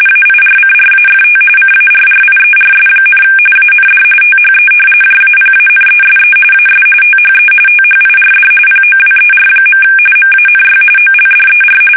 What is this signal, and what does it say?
RTTY, about 45.45 baud. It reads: AIRCRAFT CARRIERS, AND THE JAPANESE AS THE THIRD POWER. MANY OLDER
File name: fsk850